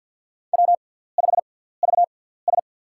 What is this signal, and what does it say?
Morse code, heard as a keyed tone, about 47 wpm, spaced as for 17 wpm, 700 Hz. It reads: W5VS
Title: USA Call Signs - 4 Characters - 6x Character Spacing - 50wpm